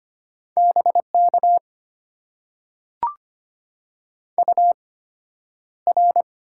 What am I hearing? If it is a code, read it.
Morse code, heard as a keyed tone, about 25 wpm, spaced as for 11 wpm, 700 Hz. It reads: BK UR
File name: Top 90 QSO Elements - 8x Character Spacing - 25wpm